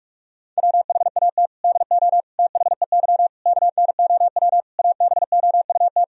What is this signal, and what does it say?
Morse code, heard as a keyed tone, about 48 wpm, 700 Hz. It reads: WHAT DO THEY KNOW ABOUT
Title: Sentences from Top 1000 Words - Review Part 3 - 45wpm